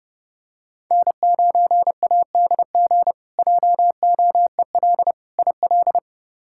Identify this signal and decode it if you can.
Morse code, heard as a keyed tone, about 30 wpm, 700 Hz. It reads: N9ADG JOEL IL
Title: K1USN SST QSO Practice - 30wpm